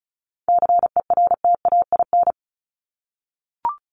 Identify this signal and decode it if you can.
Morse code, heard as a keyed tone, about 34 wpm, 700 Hz. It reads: CERTAIN
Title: Top 500 Words Repeated 35wpm